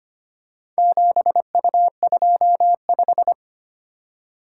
Morse code, heard as a keyed tone, about 25 wpm, 700 Hz. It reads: 7U25
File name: practice / 25wpm / Four Letter-Number Combinations - 25wpm